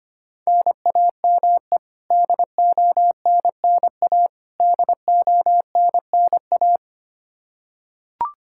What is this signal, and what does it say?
Morse code, heard as a keyed tone, about 25 wpm, 700 Hz. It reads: NAME DONNA DONNA
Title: K1USN SST QSO Practice 25wpm